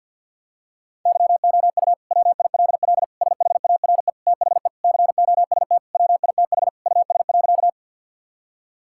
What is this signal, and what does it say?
Morse code, heard as a keyed tone, about 50 wpm, 700 Hz. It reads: YOU WILL SHARE THE COST WITH US.
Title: Sentences from Top 1000 Words - Review Part 6 - 50wpm